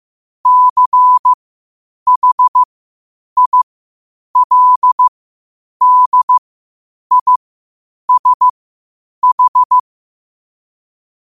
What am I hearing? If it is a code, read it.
Morse code, heard as a keyed tone, about 15 wpm, spaced as for 8 wpm, 1000 Hz. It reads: CHILDISH
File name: qotdfiles / 8wpm